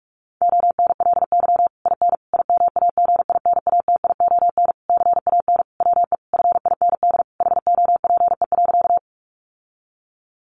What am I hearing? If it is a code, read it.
Morse code, heard as a keyed tone, about 44 wpm, 700 Hz. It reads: ONLY IN IMAGINATION CAN WE FIND HOPE.
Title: Sentences from Top 1300 Words 45wpm